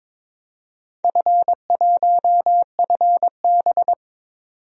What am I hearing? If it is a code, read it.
Morse code, heard as a keyed tone, about 22 wpm, 700 Hz. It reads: F1FB
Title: DXCC-100 Call Signs - 4 Characters - 22wpm